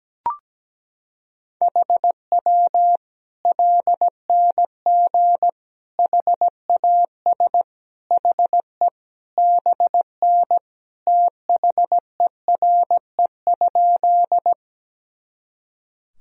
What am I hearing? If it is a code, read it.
Morse code, heard as a keyed tone, about 17 wpm, 700 Hz. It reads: HW LNG HAS HE BN THERE?
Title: Sentences from Top 100 Words and Abbreviations 17wpm